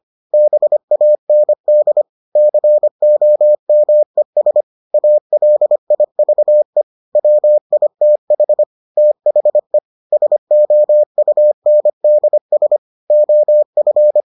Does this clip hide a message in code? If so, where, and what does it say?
Yes: Morse code from 0.3 to 14.2 s, heard as a keyed tone, about 25 wpm, 600 Hz. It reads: BAND COMES ALIVE WITH THE SOUNDS OF